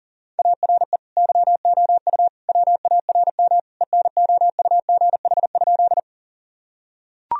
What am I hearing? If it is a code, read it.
Morse code, heard as a keyed tone, about 40 wpm, 700 Hz. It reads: ARE YOU WARM ENOUGH?
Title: Sentences from Top 1000 Words - Review Part 2 - 40wpm